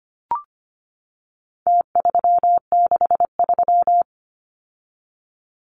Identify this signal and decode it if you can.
Morse code, heard as a keyed tone, about 25 wpm, 700 Hz. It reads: T363
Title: Numbers - Contest Style - 4-Digit 25wpm